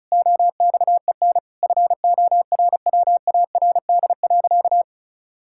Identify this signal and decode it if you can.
Morse code, heard as a keyed tone, about 34 wpm, 700 Hz. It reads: OXEN FORWARD.
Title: Pangram Challenge v2 - 35wpm